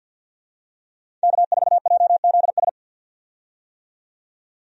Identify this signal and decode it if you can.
Morse code, heard as a keyed tone, about 51 wpm, 700 Hz. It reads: K4JZS
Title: USA Call Signs - 5 Characters - 50wpm